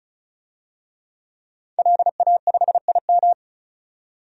Morse code, heard as a keyed tone, about 36 wpm, 700 Hz. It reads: LA5IM
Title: DXCC-100 Call Signs - 5 Characters - 35wpm